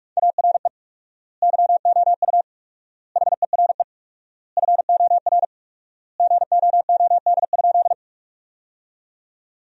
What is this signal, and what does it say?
Morse code, heard as a keyed tone, about 46 wpm, 700 Hz. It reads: ARE YOU HERE FOR GOOD?
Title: 4x Word-Spacing - Sentences from Top 200 Words - 45wpm